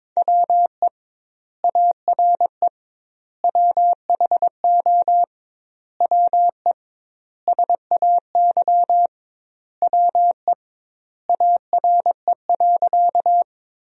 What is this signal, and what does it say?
Morse code, heard as a keyed tone, about 22 wpm, 700 Hz. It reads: WE ARE WHO WE SAY WE ARE.
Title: 2x Word-Spacing - Sentences from Top 200 Words - 22wpm